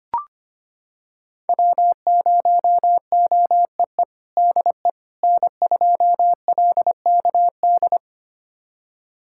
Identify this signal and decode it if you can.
Morse code, heard as a keyed tone, about 25 wpm, 700 Hz. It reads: W0OEE DE N2LKD
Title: USA Callsign DE Callsign - 5 Characters - 25wpm